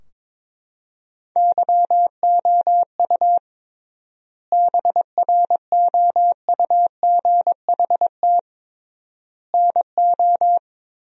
Morse code, heard as a keyed tone, about 22 wpm, 700 Hz. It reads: YOU BROUGHT NO
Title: 3x Word-Spacing - Sentences from Top 500 Words - 22wpm